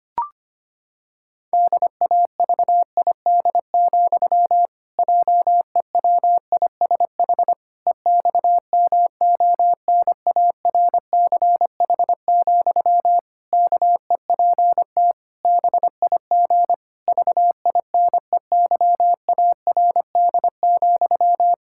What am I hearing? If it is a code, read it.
Morse code, heard as a keyed tone, about 25 wpm, 700 Hz. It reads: DAVID, JEWISH EXMONARCH, KEPT BIG VINEYARD,
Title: Pangram Challenge v2 - 25wpm